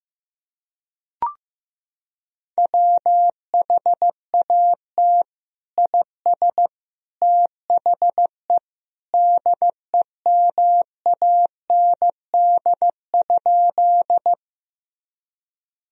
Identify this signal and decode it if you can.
Morse code, heard as a keyed tone, about 15 wpm, 700 Hz. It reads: WHAT IS THE DEMAND?
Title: Sentences from Top 1000 Words - Review Part 4 - 15wpm